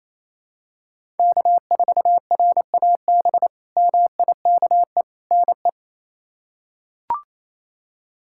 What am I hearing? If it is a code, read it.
Morse code, heard as a keyed tone, about 28 wpm, 700 Hz. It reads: K4RAB MIKE NE
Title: K1USN SST QSO Practice - 28wpm